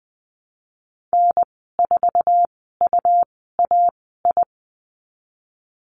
Morse code, heard as a keyed tone, about 20 wpm, spaced as for 15 wpm, 700 Hz. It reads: N4UAI
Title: 128 Lesson 21g - Number 6 - Call Signs 20wpm